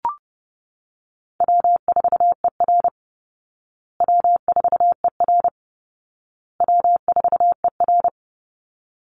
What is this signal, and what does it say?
Morse code, heard as a keyed tone, about 30 wpm, 700 Hz. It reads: W4ER W4ER W4ER
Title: Easy Call Signs Repeated 3x - 30wpm